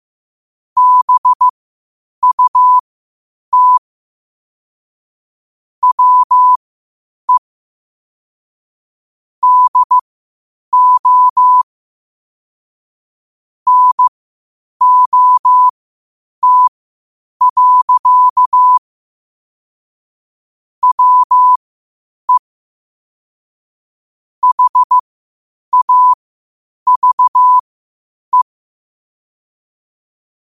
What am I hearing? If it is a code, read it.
Morse code, heard as a keyed tone, about 15 wpm, spaced as for 8 wpm, 1000 Hz. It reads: BUT WE DO NOT. WE HAVE
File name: qotdfiles / 8wpm